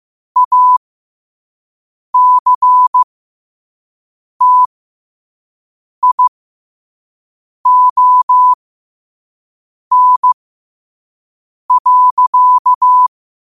Morse code, heard as a keyed tone, about 15 wpm, spaced as for 5 wpm, 1000 Hz. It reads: ACTION.
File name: qotdfiles / 5wpm